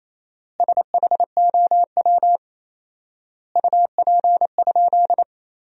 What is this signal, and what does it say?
Morse code, heard as a keyed tone, about 28 wpm, 700 Hz. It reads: SHOW UP?
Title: 4x Word-Spacing - Sentences from Top 200 Words - 28wpm